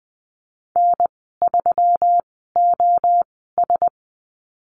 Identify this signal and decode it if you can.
Morse code, heard as a keyed tone, about 19 wpm, spaced as for 14 wpm, 700 Hz. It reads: N3OS
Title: 122 Lesson 20g - Number 3 - Call Signs 20wpm